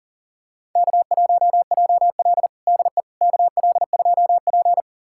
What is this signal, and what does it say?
Morse code, heard as a keyed tone, about 40 wpm, 700 Hz. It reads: K1JL DE KL2P
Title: USA Callsign DE Callsign - 4 Characters - 40wpm